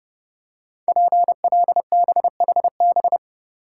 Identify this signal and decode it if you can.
Morse code, heard as a keyed tone, about 30 wpm, 700 Hz. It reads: PLBHB
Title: Code Groups - 5 Letter:Numbers - 30wpm